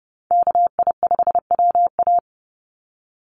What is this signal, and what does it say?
Morse code, heard as a keyed tone, about 30 wpm, 700 Hz. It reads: KI5WA
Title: NA SOTA Chaser Callsigns 30wpm